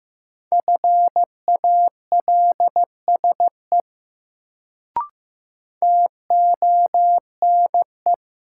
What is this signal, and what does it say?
Morse code, heard as a keyed tone, about 15 wpm, 700 Hz. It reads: FALSE TONE
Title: Sentences from Top 1600 Words - New Word Drill - 15wpm